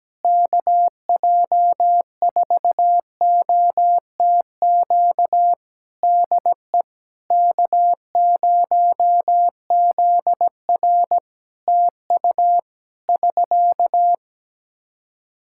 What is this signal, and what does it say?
Morse code, heard as a keyed tone, about 17 wpm, 700 Hz. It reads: KJ4OTQ DE K0ZR TU <SK>
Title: Practice QSO - Exchange 4 - Ending - 17wpm